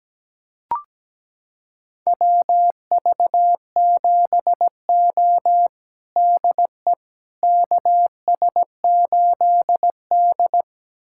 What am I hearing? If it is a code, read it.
Morse code, heard as a keyed tone, about 17 wpm, 700 Hz. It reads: WV7O DE KS8D
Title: USA Callsign DE Callsign - 4 Characters - 17wpm